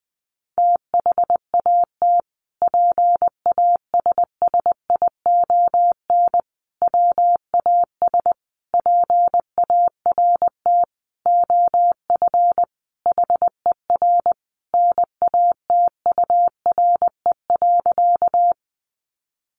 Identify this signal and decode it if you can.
Morse code, heard as a keyed tone, about 20 wpm, 700 Hz. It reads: THAT PASSION WAS PART OF HER NATURE.
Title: Sentences from Top 1300 Words 20wpm